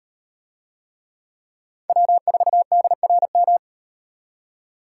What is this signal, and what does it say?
Morse code, heard as a keyed tone, about 38 wpm, 700 Hz. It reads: W4DRM